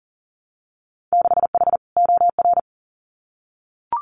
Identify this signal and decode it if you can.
Morse code, heard as a keyed tone, about 39 wpm, 700 Hz. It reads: 6H OR
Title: Winter Field Day Exchanges Encore 40wpm